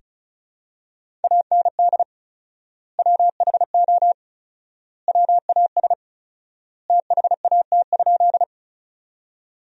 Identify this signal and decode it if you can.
Morse code, heard as a keyed tone, about 34 wpm, 700 Hz. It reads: AND WHO WAS THAT?
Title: 4x Word-Spacing - Sentences from Top 100 Words - 35wpm